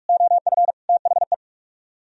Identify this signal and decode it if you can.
Morse code, heard as a keyed tone, about 58 wpm, 700 Hz. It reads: OF THE